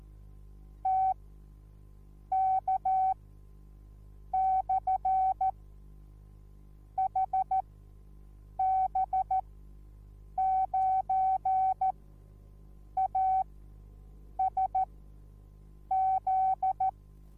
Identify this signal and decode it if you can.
Morse code, heard as a keyed tone, about 14 wpm, spaced as for 6 wpm, 760 Hz. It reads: TK/HB9ASZ